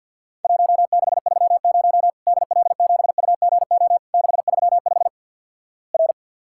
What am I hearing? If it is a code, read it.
Morse code, heard as a keyed tone, about 50 wpm, 700 Hz. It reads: 1630 DL7UGO 635 R
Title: Worked All Europe DX Contest - QTC Exchange - 50wpm